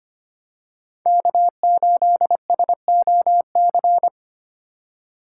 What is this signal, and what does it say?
Morse code, heard as a keyed tone, about 29 wpm, 700 Hz. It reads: K8SOC
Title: Speed Racing - USA Call Signs 25wpm